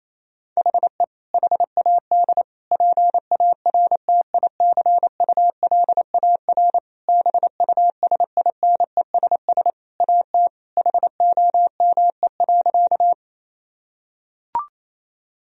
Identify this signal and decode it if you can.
Morse code, heard as a keyed tone, about 28 wpm, 700 Hz. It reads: HE HAD PARTICULAR BUSINESS AT HOME.
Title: Sentences from Top 1000 Words - Review Part 4 - 28wpm